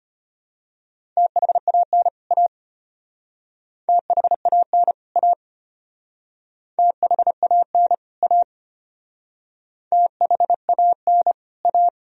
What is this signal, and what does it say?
Morse code, heard as a keyed tone, about 31 wpm, 700 Hz. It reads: THAN A THAN A THAN A THAN A
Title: Speed-Racing - Sets of 2 Words 25wpm